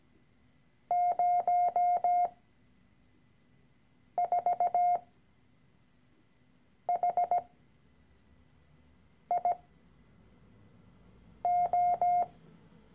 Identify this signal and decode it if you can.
Morse code, heard as a keyed tone, about 17 wpm, spaced as for 4 wpm, 710 Hz. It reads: 04HIO
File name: CW-Online-Kurs / Anfänger / Lektion 013